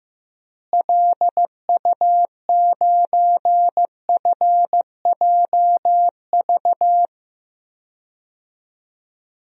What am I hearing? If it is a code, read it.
Morse code, heard as a keyed tone, about 15 wpm, 700 Hz. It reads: LU9FJV
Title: DXCC-100 Call Signs - 6 Characters - Repeated 3x - 15wpm